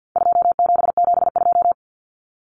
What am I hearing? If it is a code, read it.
Morse code, heard as a keyed tone, about 50 wpm, 700 Hz. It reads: 2772
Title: Common Cut Numbers - 4-Digit 50wpm